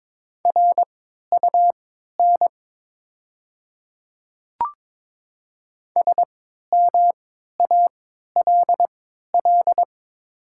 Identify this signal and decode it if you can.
Morse code, heard as a keyed tone, about 22 wpm, spaced as for 13 wpm, 690 Hz. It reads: RUN SMALL
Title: 3x Character Spacing - Top 300 Words - 22wpm